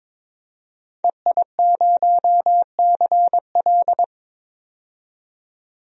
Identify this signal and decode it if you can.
Morse code, heard as a keyed tone, about 22 wpm, 700 Hz. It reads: EI0CL
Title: DXCC-100 Call Signs - 5 Characters - 22wpm